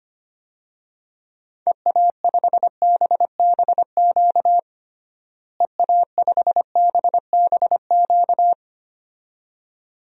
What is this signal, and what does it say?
Morse code, heard as a keyed tone, about 25 wpm, 700 Hz. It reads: EA5BBQ EA5BBQ
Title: DXCC-100 Call Signs - 6 Characters - Repeated 2x - 25wpm